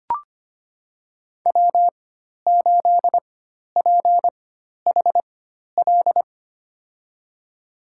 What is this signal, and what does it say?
Morse code, heard as a keyed tone, about 25 wpm, spaced as for 12 wpm, 700 Hz. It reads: W8PHL
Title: USA Call Signs - 5 Characters - 4x Character Spacing - 25wpm